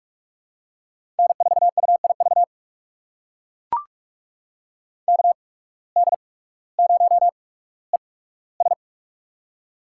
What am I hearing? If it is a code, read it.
Morse code, heard as a keyed tone, about 44 wpm, spaced as for 19 wpm, 700 Hz. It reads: N4UIV KD0ES
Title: USA Call Signs - 5 Characters - 8x Character Spacing - 45wpm